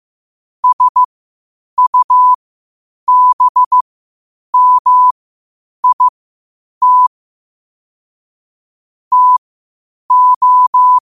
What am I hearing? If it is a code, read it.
Morse code, heard as a keyed tone, about 15 wpm, spaced as for 8 wpm, 1000 Hz. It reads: SUBMIT TO
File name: qotdfiles / 8wpm